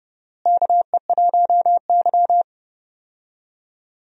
Morse code, heard as a keyed tone, about 30 wpm, 700 Hz. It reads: KE1Y
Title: USA Call Signs - 4 Characters - Repeated 2x - 30wpm